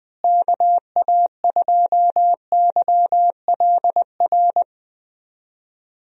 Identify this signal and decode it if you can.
Morse code, heard as a keyed tone, about 20 wpm, 700 Hz. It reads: KA2YLR
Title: USA Call Signs - 6 Characters - Repeated 3x - 20wpm